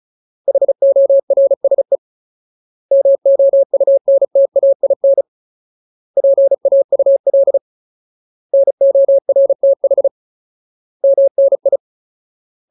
Morse code, heard as a keyed tone, about 34 wpm, 550 Hz. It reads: HORSE MOUNTAIN PAUL NORTH MNI
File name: Ultimate Word List / 35WPM Ultimate